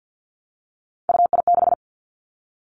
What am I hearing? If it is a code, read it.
Morse code, heard as a keyed tone, about 50 wpm, 700 Hz. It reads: UI6
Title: Three Challenging Letter-Number Combinations 50wpm